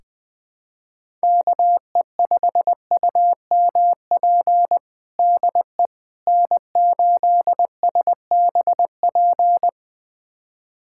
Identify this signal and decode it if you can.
Morse code, heard as a keyed tone, about 20 wpm, 700 Hz. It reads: KE5UMP DE N8SBP
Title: USA Callsign DE Callsign 20wpm